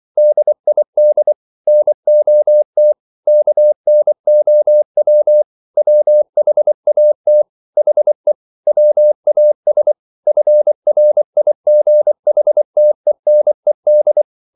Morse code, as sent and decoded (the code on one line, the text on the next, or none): -.. .. -.. / -. --- - / -.- -. --- .-- / .-- .... .- - / .... / .-- .- ... / ..-. .-. .. --. .... - . -. -..
DID NOT KNOW WHAT HE WAS FRIGHTENED